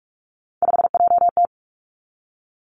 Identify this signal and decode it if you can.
Morse code, heard as a keyed tone, about 46 wpm, 700 Hz. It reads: 5JT
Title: Three Challenging Letter-Number Combinations 45wpm